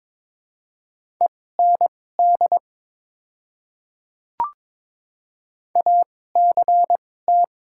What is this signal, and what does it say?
Morse code, heard as a keyed tone, about 22 wpm, spaced as for 16 wpm, 700 Hz. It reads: END ACT
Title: 2x Character Spacing - Top 500 Words - 22wpm